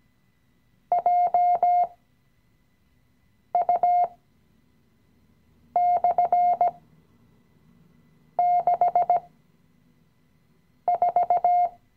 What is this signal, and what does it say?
Morse code, heard as a keyed tone, about 17 wpm, spaced as for 5 wpm, 710 Hz. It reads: JU/64
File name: CW-Online-Kurs / Anfänger / Lektion 017